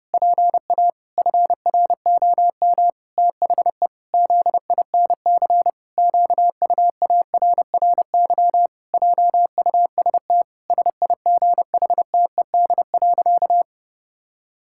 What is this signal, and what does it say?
Morse code, heard as a keyed tone, about 30 wpm, 700 Hz. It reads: PA FROM THE ZINC QUARRY JUST SIGHTED.
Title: Pangram Challenge v2 - 30wpm